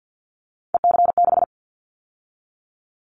Morse code, heard as a keyed tone, about 50 wpm, 700 Hz. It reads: EC6
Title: Callsign Prefixes for Top 100 DXCC - 50wpm